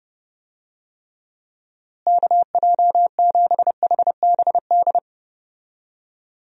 Morse code, heard as a keyed tone, about 30 wpm, 700 Hz. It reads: KJ7HBD
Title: USA Call Signs - 6 Characters - Repeated 2x - 30wpm